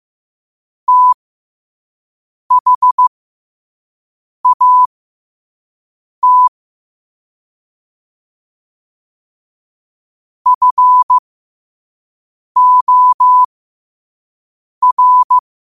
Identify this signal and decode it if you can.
Morse code, heard as a keyed tone, about 15 wpm, spaced as for 5 wpm, 1000 Hz. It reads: THAT FOR
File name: qotdfiles / 5wpm